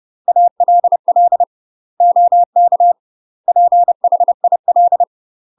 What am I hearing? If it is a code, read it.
Morse code, heard as a keyed tone, about 30 wpm, 700 Hz. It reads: ALL OK PHIL
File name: UK MASTER QSOs TRAINING / 30WPM / UK MASTER QSOs 0011